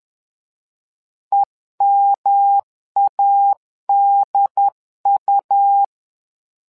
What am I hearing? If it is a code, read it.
Morse code, heard as a keyed tone, about 11 wpm, 800 Hz. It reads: EMADU
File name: LEZIONI / 3 MWRO / 3 MWRO 10wpm 2148